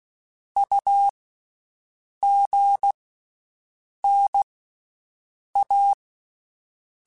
Morse code, heard as a keyed tone, about 16 wpm, 800 Hz. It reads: U G N A
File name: Basic Letter Groups By7Groups / AWUNDG / AWUNDG@10